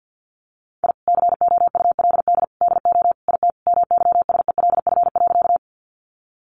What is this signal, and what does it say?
Morse code, heard as a keyed tone, about 50 wpm, 700 Hz. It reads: I COULD DO IT MYSELF.
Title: Sentences from Top 1300 Words 50wpm